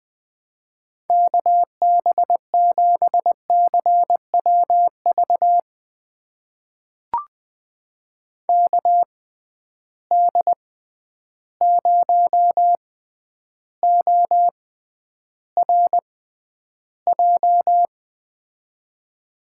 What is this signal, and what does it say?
Morse code, heard as a keyed tone, about 20 wpm, spaced as for 10 wpm, 700 Hz. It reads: KB7CWV KD0ORJ
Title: USA Call Signs - 6 Characters - 6x Character Spacing - 20wpm